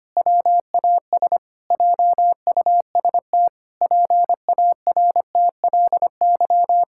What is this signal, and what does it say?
Morse code, heard as a keyed tone, about 25 wpm, 700 Hz. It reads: WAS JUST PARTLY
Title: Pangram Challenge v2 - 25wpm